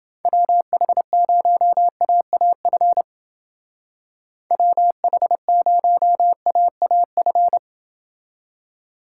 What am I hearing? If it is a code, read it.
Morse code, heard as a keyed tone, about 29 wpm, 700 Hz. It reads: WH0AAF WH0AAF